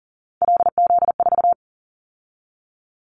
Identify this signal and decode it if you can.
Morse code, heard as a keyed tone, about 42 wpm, 700 Hz. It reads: LZ4
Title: Callsign Prefixes for Top 50 DXCC - 40wpm